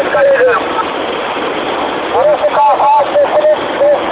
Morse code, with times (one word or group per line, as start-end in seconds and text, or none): none